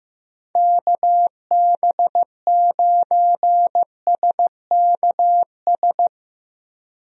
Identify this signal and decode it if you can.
Morse code, heard as a keyed tone, about 15 wpm, 700 Hz. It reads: KB9SKS
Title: USA Call Signs - 6 Characters - Repeated 2x - 15wpm